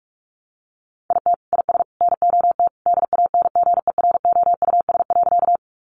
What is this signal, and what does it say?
Morse code, heard as a keyed tone, about 44 wpm, 700 Hz. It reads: IT IS NOT DANGEROUS.
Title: Sentences from Top 1200 Words 45wpm